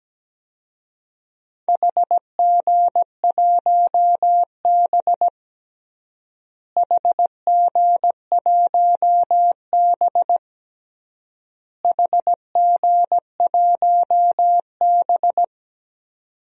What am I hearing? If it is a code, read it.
Morse code, heard as a keyed tone, about 17 wpm, 700 Hz. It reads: HG1B HG1B HG1B
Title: DXCC-100 Call Signs - 4 Characters - Repeated 3x - 17wpm